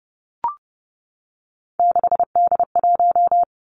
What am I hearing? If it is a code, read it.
Morse code, heard as a keyed tone, about 32 wpm, 700 Hz. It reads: 6D1
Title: Three Challenging Letter-Number Combinations 30wpm